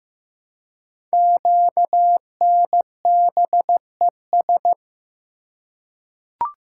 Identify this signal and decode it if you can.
Morse code, heard as a keyed tone, about 15 wpm, 700 Hz. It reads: QNBES
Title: Code Groups - 5 Letters - 15wpm